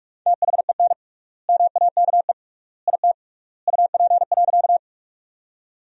Morse code, heard as a keyed tone, about 42 wpm, 700 Hz. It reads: THEN MAKE IT UP.
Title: 3x Word-Spacing - Sentences from Top 100 Words - 45wpm